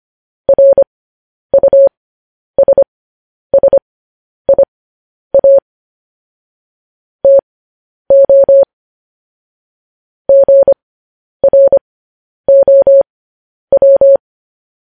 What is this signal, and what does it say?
Morse code, heard as a keyed tone, about 25 wpm, spaced as for 10 wpm, 550 Hz. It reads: RUSSIA TO GROW
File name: podcast / News headlines-1x-25x10